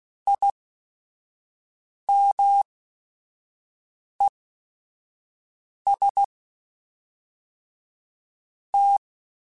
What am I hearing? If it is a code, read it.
Morse code, heard as a keyed tone, about 16 wpm, 800 Hz. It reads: I M E S T
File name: Technical Subjects / Technical Subjects-mp3s / XC @5